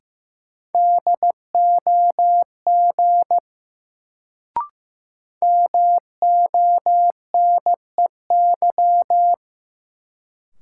Morse code, heard as a keyed tone, about 15 wpm, 700 Hz. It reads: DOG MONEY
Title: Sentences from Top 400 Words - New Word Drill - 15wpm